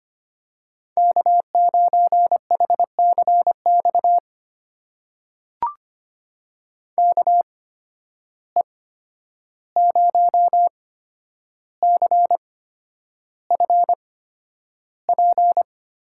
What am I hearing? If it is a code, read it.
Morse code, heard as a keyed tone, about 25 wpm, spaced as for 10 wpm, 700 Hz. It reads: K9HCX KE0CFP
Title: USA Call Signs - 8x Character Spacing - 25wpm